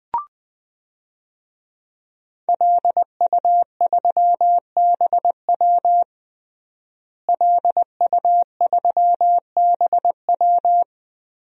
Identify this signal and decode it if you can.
Morse code, heard as a keyed tone, about 20 wpm, 700 Hz. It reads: LU3BW LU3BW
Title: DXCC-100 Call Signs - 5 Characters - Repeated 3x - 20wpm